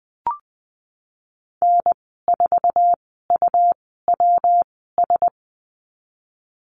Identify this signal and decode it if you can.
Morse code, heard as a keyed tone, about 20 wpm, spaced as for 15 wpm, 700 Hz. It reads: N4UWS
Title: 128 Lesson 21g - Number 6 - Call Signs 20wpm